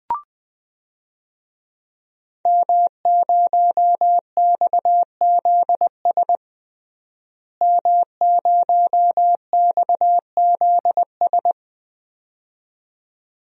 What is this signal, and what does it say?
Morse code, heard as a keyed tone, about 20 wpm, 700 Hz. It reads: M0XZS M0XZS
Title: DXCC-100 Call Signs - 5 Characters - Repeated 2x - 20wpm